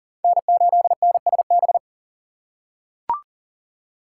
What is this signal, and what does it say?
Morse code, heard as a keyed tone, about 40 wpm, 700 Hz. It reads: N8NSB
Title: USA Call Signs - 5 Characters - 4x Character Spacing - 40wpm